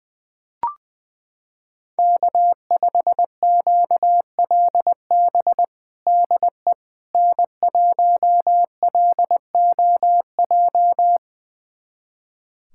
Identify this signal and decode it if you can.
Morse code, heard as a keyed tone, about 20 wpm, 700 Hz. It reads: K5QLB DE N1LOJ
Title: USA Callsign DE Callsign - 5 Characters - 20wpm